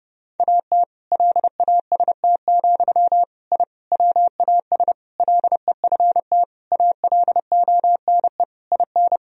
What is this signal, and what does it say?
Morse code, heard as a keyed tone, about 30 wpm, 700 Hz. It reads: AT LAST, I WAS LEFT ALONE IN
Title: Sentences from Top 1000 Words - Review Part 1 - 30wpm